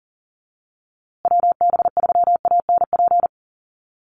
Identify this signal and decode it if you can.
Morse code, heard as a keyed tone, about 40 wpm, 700 Hz. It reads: WB3ANP
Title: USA Call Signs Repeated 2x 40wpm